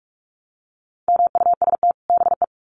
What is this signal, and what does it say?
Morse code, heard as a keyed tone, about 44 wpm, 700 Hz. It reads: MUST BE
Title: Sets of 2 Words 45wpm